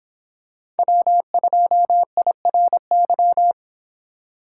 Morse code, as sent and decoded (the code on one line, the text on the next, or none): .-- ..--- .. .-. -.--
W2IRY